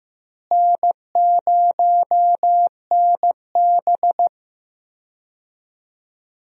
Morse code, heard as a keyed tone, about 15 wpm, 700 Hz. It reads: N0NB
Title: USA Call Signs - 4 Characters - Repeated 3x - 15wpm